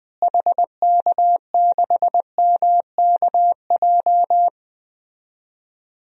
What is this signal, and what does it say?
Morse code, heard as a keyed tone, about 20 wpm, 700 Hz. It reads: HK6MKJ
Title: DXCC-100 Call Signs - 6 Characters - Repeated 3x - 20wpm